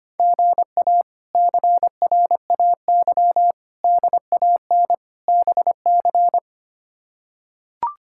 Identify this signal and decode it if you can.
Morse code, heard as a keyed tone, about 25 wpm, 700 Hz. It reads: GA CRAY DAN BC
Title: K1USN SST QSO Practice - 25wpm